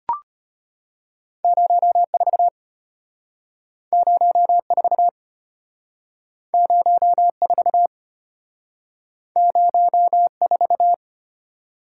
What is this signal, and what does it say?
Morse code, heard as a keyed tone, about 34 wpm, 700 Hz. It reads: 04 04 04 04